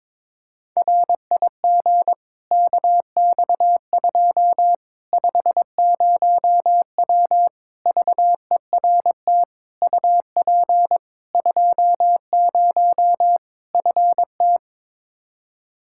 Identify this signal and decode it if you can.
Morse code, heard as a keyed tone, about 22 wpm, 700 Hz. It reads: RIG KX2 50W VERT UP 20 FT
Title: Practice QSO - Exchange 2 - Info - 22wpm